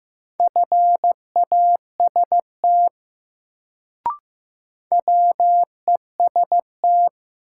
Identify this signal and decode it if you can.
Morse code, heard as a keyed tone, about 15 wpm, 700 Hz. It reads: FAST WEST
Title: Sentences from Top 400 Words - New Word Drill - 15wpm